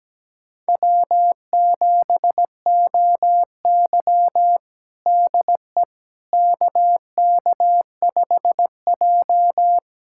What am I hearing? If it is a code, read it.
Morse code, heard as a keyed tone, about 17 wpm, 700 Hz. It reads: W7OY DE KK5J
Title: USA Callsign DE Callsign - 4 Characters - 17wpm